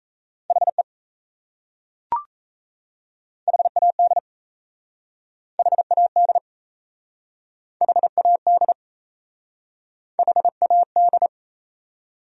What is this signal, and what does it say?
Morse code, heard as a keyed tone, about 37 wpm, 700 Hz. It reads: HE HAD HAD HAD HAD